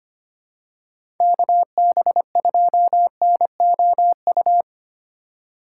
Morse code, heard as a keyed tone, about 25 wpm, 700 Hz. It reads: KB2NOU
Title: USA Call Signs - 6 Characters - 2x Character Spacing - 25wpm